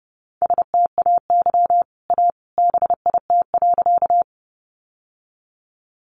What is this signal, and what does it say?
Morse code, heard as a keyed tone, about 30 wpm, 700 Hz. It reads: STAY A BIT.
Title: Sentences from Top 900 Words 30wpm